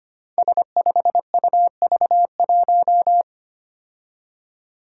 Morse code, heard as a keyed tone, about 25 wpm, 700 Hz. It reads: S5UV1
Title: Five Challenging Letter-Number Combinations 25wpm